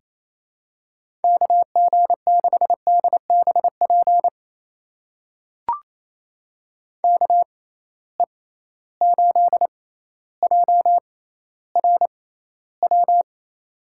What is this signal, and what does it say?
Morse code, heard as a keyed tone, about 28 wpm, spaced as for 14 wpm, 700 Hz. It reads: KG6DBP KE8JRW
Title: USA Call Signs - 6x Character Spacing - 28wpm